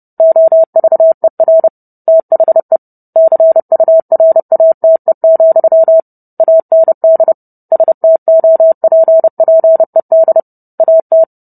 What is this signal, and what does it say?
Morse code, heard as a keyed tone, about 30 wpm, 650 Hz. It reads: OVER THE CURATE, AND STOPPED AT